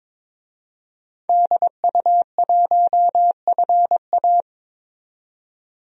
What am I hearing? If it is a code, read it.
Morse code, heard as a keyed tone, about 22 wpm, 700 Hz. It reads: DU1FA
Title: DXCC-100 Call Signs - 5 Characters - 22wpm